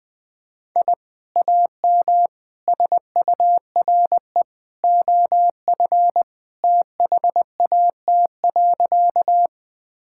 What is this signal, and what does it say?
Morse code, heard as a keyed tone, about 21 wpm, 700 Hz. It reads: I AM SURE OF THAT.
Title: Sentences with 2-Word Phrases - 20wpm